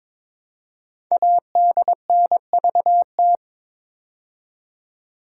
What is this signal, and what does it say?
Morse code, heard as a keyed tone, about 22 wpm, 700 Hz. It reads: ADNVT
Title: Code Groups - 5 Letters - 22wpm